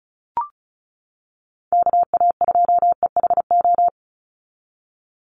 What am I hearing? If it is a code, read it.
Morse code, heard as a keyed tone, about 34 wpm, 700 Hz. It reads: KA2EHO
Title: USA Call Signs 35wpm